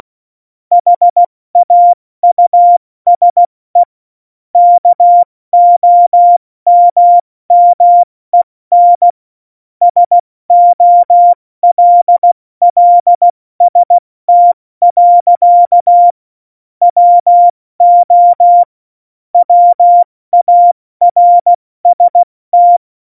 Morse code, heard as a keyed tone, about 16 wpm, spaced as for 14 wpm, 700 Hz. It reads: HAUSE KOMMEN SOLLST. WO WARST
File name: Morsekurs / Witze / Witz 29. Oktober 2019, 70 F80